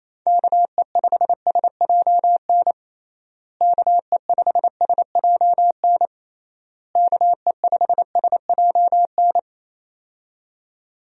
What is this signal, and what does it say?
Morse code, heard as a keyed tone, about 28 wpm, 700 Hz. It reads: KE5SJN KE5SJN KE5SJN
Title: USA Call Signs - 6 Characters - Repeated 3x - 28wpm